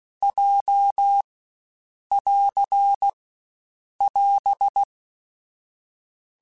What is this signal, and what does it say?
Morse code, heard as a keyed tone, about 16 wpm, 800 Hz. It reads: J <AR> <AS>
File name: Basic Letter Groups By7Groups / JZ17 / JZ17@15